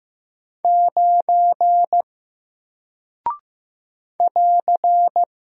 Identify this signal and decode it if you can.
Morse code, heard as a keyed tone, about 15 wpm, 700 Hz. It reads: 9 <AR>
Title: Single Character (Letter:Number:Punctuation:Prosign) Alt 15wpm